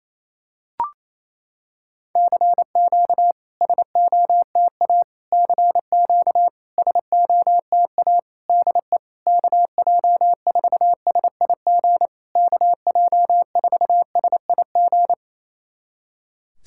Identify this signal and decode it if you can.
Morse code, heard as a keyed tone, about 28 wpm, 700 Hz. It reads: CQ SOTA CQ SOTA DE KJ4SIG KJ4SIG
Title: SOTA QSO Practice 28wpm